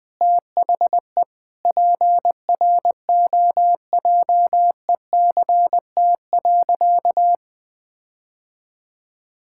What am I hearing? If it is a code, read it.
Morse code, heard as a keyed tone, about 20 wpm, 700 Hz. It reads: THE PROJECT.
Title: Sentences with 6-Word Phrases - 20wpm